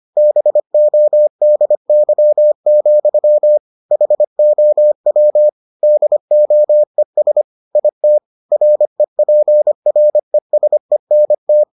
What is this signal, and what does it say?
Morse code, heard as a keyed tone, about 25 wpm, 600 Hz. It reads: BODY, HOW DOES IT REPRESENT